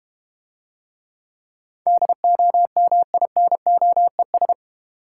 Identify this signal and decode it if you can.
Morse code, heard as a keyed tone, about 30 wpm, 700 Hz. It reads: DOMINOES
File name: Easy Words - Speed-Racing with Hints - 25wpm